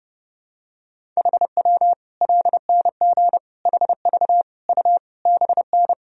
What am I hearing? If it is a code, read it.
Morse code, heard as a keyed tone, about 30 wpm, 700 Hz. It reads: HW LNG HV U BN
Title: Sentences from Top 200 Words and Abbreviations 30wpm